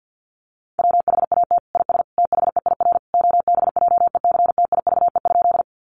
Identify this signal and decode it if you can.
Morse code, heard as a keyed tone, about 50 wpm, 700 Hz. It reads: WHAT IS THEIR OBJECTIVE?
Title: Sentences from Top 1200 Words 50wpm